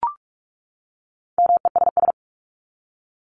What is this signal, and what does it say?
Morse code, heard as a keyed tone, about 44 wpm, 700 Hz. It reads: MESS
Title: Top 1751-2000 Words 45wpm